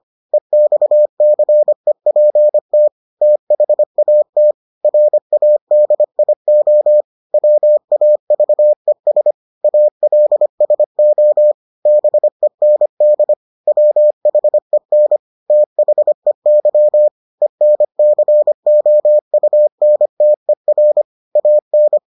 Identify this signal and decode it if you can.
Morse code, heard as a keyed tone, about 25 wpm, 600 Hz. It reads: EXCEPT THAT RADIO WAVES ALSO BEND WHEN THEY ENCOUNTER AN